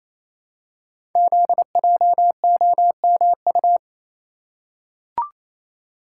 Morse code, heard as a keyed tone, about 27 wpm, 700 Hz. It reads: ZJOMU
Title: Code Groups - 5 Letters - 28wpm